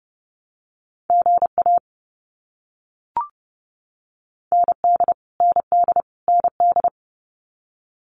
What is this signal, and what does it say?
Morse code, heard as a keyed tone, about 30 wpm, 700 Hz. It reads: GA ND ND ND
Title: US States Repeated 3x 30wpm